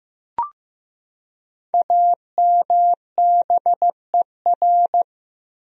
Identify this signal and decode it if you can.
Morse code, heard as a keyed tone, about 15 wpm, 700 Hz. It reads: AMBER
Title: USA Common Names - Max 5 Letters - 15wpm